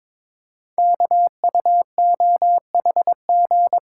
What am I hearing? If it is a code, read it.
Morse code, heard as a keyed tone, about 22 wpm, 700 Hz. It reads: KUOHG
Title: Code Groups - 5 Letters - 22wpm